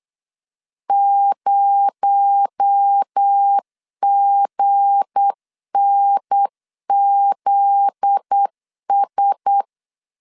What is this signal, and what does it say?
Morse code, heard as a keyed tone, about 8 wpm, 800 Hz. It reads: 0GNZS
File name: LEZIONI / 7 GF9Z / 7 GF9Z 8wpm 3213